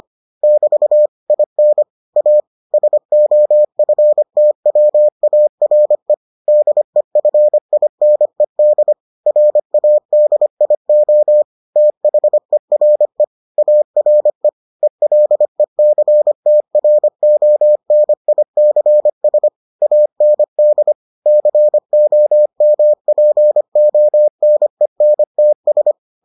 Morse code, heard as a keyed tone, about 25 wpm, 600 Hz. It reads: <BT> IN A SOFTWARE DEFINED RADIO THERE ARE ELECTRONICS AND COMPONENTS